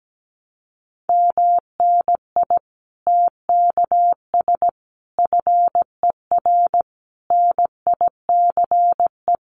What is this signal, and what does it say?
Morse code, heard as a keyed tone, about 17 wpm, 700 Hz. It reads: MNI TKS FER NICE
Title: Common QSO Phrases 17wpm